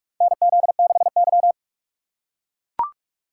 Morse code, heard as a keyed tone, about 45 wpm, 700 Hz. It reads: NZ6Y
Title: USA Call Signs - 4 Characters - 2x Character Spacing - 45wpm